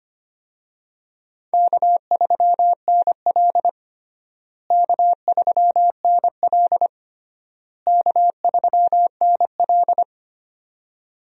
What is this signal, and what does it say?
Morse code, heard as a keyed tone, about 25 wpm, 700 Hz. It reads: K3NL K3NL K3NL
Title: USA Call Signs - 4 Characters - Repeated 3x - 25wpm